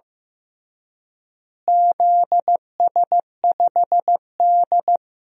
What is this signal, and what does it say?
Morse code, heard as a keyed tone, about 15 wpm, 700 Hz. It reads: ZS5D
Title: DXCC-100 Call Signs - 4 Characters - Repeated 3x - 15wpm